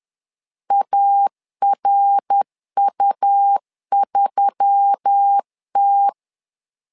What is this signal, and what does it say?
Morse code, heard as a keyed tone, about 10 wpm, 800 Hz. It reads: ARU3T